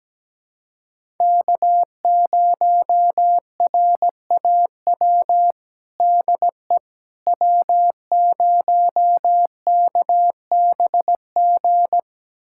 Morse code, heard as a keyed tone, about 17 wpm, 700 Hz. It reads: K0RAW DE W0KBG
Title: USA Callsign DE Callsign - 5 Characters - 17wpm